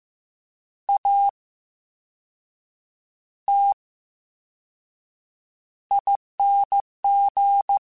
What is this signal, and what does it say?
Morse code, heard as a keyed tone, about 15 wpm, 800 Hz. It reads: A T I N G
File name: Special INGs-mp3s / Special INGs-mp3s / VerySlowToFaster Ing @13